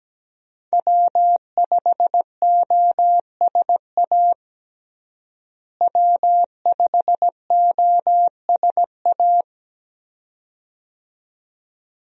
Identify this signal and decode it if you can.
Morse code, heard as a keyed tone, about 17 wpm, 700 Hz. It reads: W5OSA W5OSA
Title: USA Call Signs - 5 Characters - Repeated 2x - 17wpm